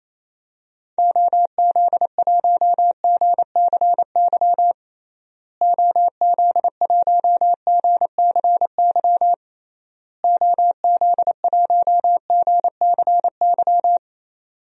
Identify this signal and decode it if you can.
Morse code, heard as a keyed tone, about 28 wpm, 700 Hz. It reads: OZ1GCY OZ1GCY OZ1GCY
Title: DXCC-100 Call Signs - 6 Characters - Repeated 3x - 28wpm